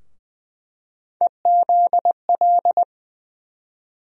Morse code, heard as a keyed tone, about 21 wpm, 700 Hz. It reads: EZL
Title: USA Call Sign Postfixes - 3 Letters 20wpm